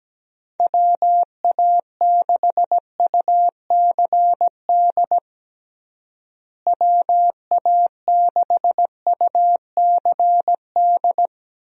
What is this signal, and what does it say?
Morse code, heard as a keyed tone, about 17 wpm, 700 Hz. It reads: WA6UCD WA6UCD
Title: USA Call Signs - 6 Characters - Repeated 3x - 17wpm